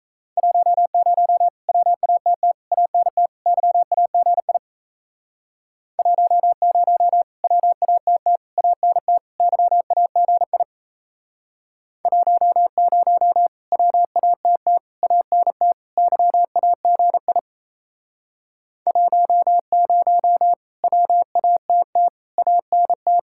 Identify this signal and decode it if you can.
Morse code, heard as a keyed tone, about 35 wpm, 700 Hz. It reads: 10 WATT ANT YAGI 10 WATT ANT YAGI 10 WATT ANT YAGI 10 WATT ANT
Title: Speed-Racing Common QSO Phrases - 28wpm